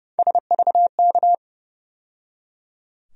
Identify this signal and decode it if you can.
Morse code, heard as a keyed tone, about 30 wpm, 700 Hz. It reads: SVK
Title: USA Call Sign Postfixes - 3 Letters 30wpm